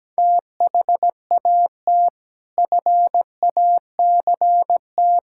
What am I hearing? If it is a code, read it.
Morse code, heard as a keyed tone, about 17 wpm, 700 Hz. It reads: THAT FACT
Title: Sentences from Top 1000 Words - Review Part 6 - 17wpm